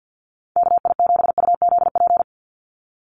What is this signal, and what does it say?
Morse code, heard as a keyed tone, about 50 wpm, 700 Hz. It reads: KI7UZP
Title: USA Call Signs 50wpm